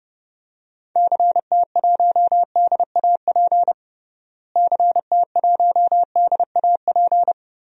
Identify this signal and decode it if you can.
Morse code, heard as a keyed tone, about 30 wpm, 700 Hz. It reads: CT1DAP CT1DAP
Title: DXCC-100 Call Signs - 6 Characters - Repeated 3x - 30wpm